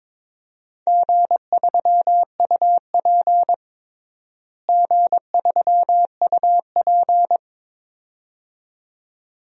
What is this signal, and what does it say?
Morse code, heard as a keyed tone, about 22 wpm, 700 Hz. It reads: G3UP G3UP
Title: DXCC-100 Call Signs - 4 Characters - Repeated 2x - 22wpm